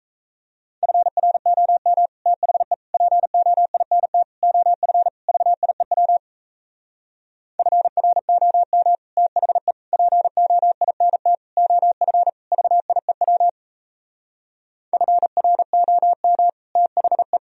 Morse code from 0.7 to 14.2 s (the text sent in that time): FROM THE POINT OF VIEW FROM THE POINT OF VIEW